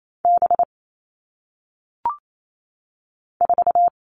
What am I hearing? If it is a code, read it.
Morse code, heard as a keyed tone, about 28 wpm, 700 Hz. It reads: B 4
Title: All Cut Numbers - 1-Digit 28wpm